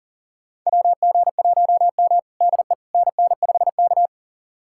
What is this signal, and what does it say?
Morse code, heard as a keyed tone, about 37 wpm, 700 Hz. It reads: WG1M DE NN5K
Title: USA Callsign DE Callsign - 4 Characters - 40wpm